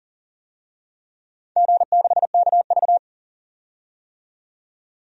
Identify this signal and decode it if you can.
Morse code, heard as a keyed tone, about 40 wpm, 700 Hz. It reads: G6KV
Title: DXCC-100 Call Signs - 4 Characters - 40wpm